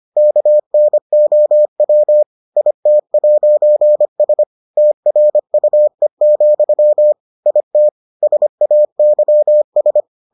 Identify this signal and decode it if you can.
Morse code, heard as a keyed tone, about 25 wpm, 600 Hz. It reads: KNOW IT'S TRUE, IT SAYS